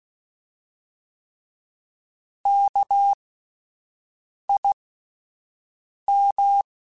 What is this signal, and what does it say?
Morse code, heard as a keyed tone, about 16 wpm, spaced as for 5 wpm, 800 Hz. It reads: KIM
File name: Alphabetical / Alphabetical mp3s / Ks @5